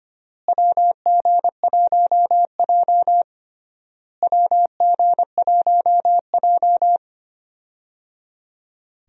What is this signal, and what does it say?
Morse code, heard as a keyed tone, about 25 wpm, 700 Hz. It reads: WG1J WG1J
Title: USA Call Signs - 4 Characters - Repeated 3x - 25wpm